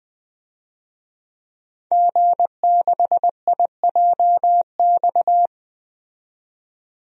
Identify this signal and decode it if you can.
Morse code, heard as a keyed tone, about 20 wpm, 700 Hz. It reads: G6IJX
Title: DXCC-100 Call Signs - 5 Characters - 20wpm